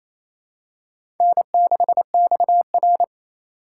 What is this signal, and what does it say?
Morse code, heard as a keyed tone, about 28 wpm, 700 Hz. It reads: N6XR
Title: USA Call Signs - 2x Character Spacing - 28wpm